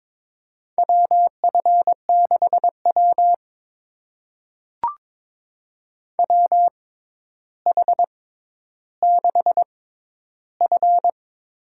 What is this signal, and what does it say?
Morse code, heard as a keyed tone, about 22 wpm, spaced as for 11 wpm, 700 Hz. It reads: WF6W WH6F
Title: USA Call Signs - 4 Characters - 6x Character Spacing - 22wpm